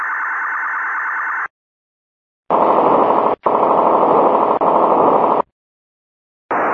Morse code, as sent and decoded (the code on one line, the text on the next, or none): none